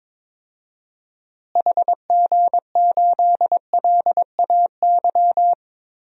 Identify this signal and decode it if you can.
Morse code, heard as a keyed tone, about 21 wpm, 700 Hz. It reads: HG8LAY
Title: DXCC-100 Call Signs - 6 Characters - Repeated 2x - 22wpm